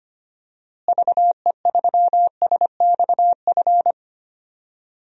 Morse code, heard as a keyed tone, about 25 wpm, 700 Hz. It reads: VE3SXF
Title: DXCC-100 Call Signs - 6 Characters - Repeated 2x - 25wpm